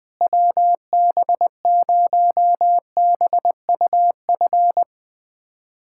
Morse code, heard as a keyed tone, about 20 wpm, 700 Hz. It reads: WB0BUF
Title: USA Callsign DE Callsign - 6 Characters - 20wpm